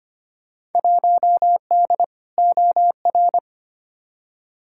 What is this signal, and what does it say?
Morse code, heard as a keyed tone, about 25 wpm, 700 Hz. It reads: 1D OR
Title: Field Day Exchanges - 25wpm